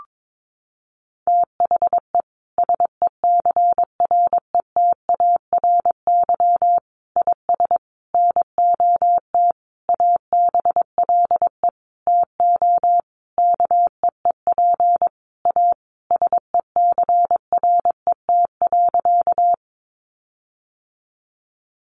Morse code, heard as a keyed tone, about 22 wpm, 700 Hz. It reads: THE SECRETARY IS NOT ABLE TO KEEP A SECRET.
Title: Sentences from Top 1300 Words 22wpm